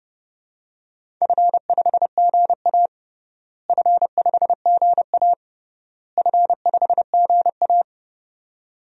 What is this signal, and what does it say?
Morse code, heard as a keyed tone, about 30 wpm, 700 Hz. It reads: F5GA F5GA F5GA
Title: DXCC-100 Call Signs - 4 Characters - Repeated 3x - 30wpm